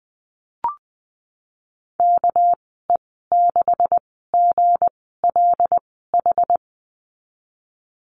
Morse code, heard as a keyed tone, about 20 wpm, spaced as for 15 wpm, 700 Hz. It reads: KE6GLH
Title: 231 Lesson 37g - Letter x - Call Signs 20wpm